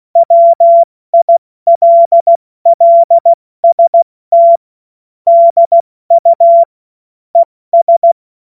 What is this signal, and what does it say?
Morse code, heard as a keyed tone, about 16 wpm, 680 Hz. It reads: WILLST DU ES
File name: Morsekurs / Witze / Witz 01. November 2022 70 F80